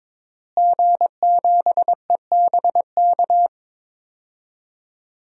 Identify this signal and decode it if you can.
Morse code, heard as a keyed tone, about 22 wpm, 700 Hz. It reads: G7EBK
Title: DXCC-100 Call Signs - 5 Characters - Repeated 2x - 22wpm